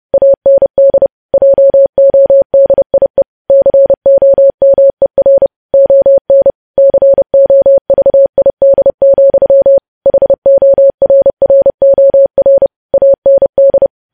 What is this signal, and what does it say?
Morse code, heard as a keyed tone, about 30 wpm, 550 Hz. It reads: AND JODIE COMER ON COVID, HORROR AND
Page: news headlines in morse code at 30 wpm